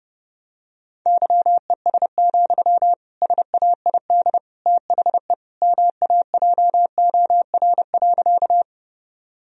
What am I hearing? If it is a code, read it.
Morse code, heard as a keyed tone, about 30 wpm, 700 Hz. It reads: YES, SAID THE MAJOR.
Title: Sentences from Top 1000 Words - Review Part 4 - 30wpm